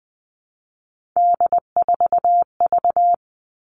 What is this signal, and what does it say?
Morse code, heard as a keyed tone, about 20 wpm, 700 Hz. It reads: D4V
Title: All Cut Numbers - 3-Digit 20wpm